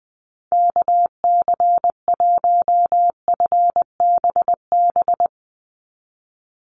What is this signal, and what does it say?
Morse code, heard as a keyed tone, about 20 wpm, 700 Hz. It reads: KC1FBB
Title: USA Call Signs 20wpm